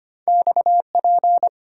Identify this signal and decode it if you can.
Morse code, heard as a keyed tone, about 34 wpm, 690 Hz. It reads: XP